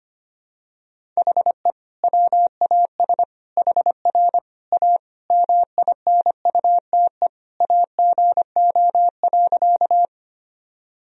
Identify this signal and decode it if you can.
Morse code, heard as a keyed tone, about 25 wpm, 700 Hz. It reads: HE WAS HR A MINUTE AGO.
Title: Sentences from Top 500 Words and Abbreviations 25wpm